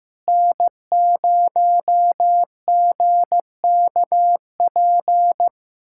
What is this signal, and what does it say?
Morse code, heard as a keyed tone, about 15 wpm, 700 Hz. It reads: N0GKP
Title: USA Callsign DE Callsign - 5 Characters - 15wpm